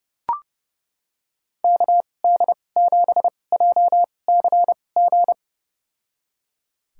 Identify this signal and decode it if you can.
Morse code, heard as a keyed tone, about 29 wpm, spaced as for 21 wpm, 700 Hz. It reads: KD7JCG
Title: USA Call Signs - 2x Character Spacing - 30wpm